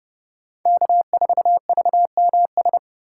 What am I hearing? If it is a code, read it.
Morse code, heard as a keyed tone, about 29 wpm, 700 Hz. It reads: K4VMS
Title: USA Call Signs - 5 Characters - 4x Character Spacing - 30wpm